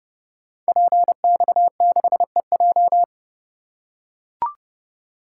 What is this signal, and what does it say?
Morse code, heard as a keyed tone, about 29 wpm, 700 Hz. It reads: PX6EJ
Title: Code Groups - 5 Letter:Numbers - 30wpm